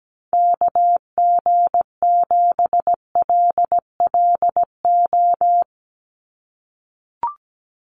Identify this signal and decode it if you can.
Morse code, heard as a keyed tone, about 17 wpm, 700 Hz. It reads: KG7LLO
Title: USA Call Signs 17wpm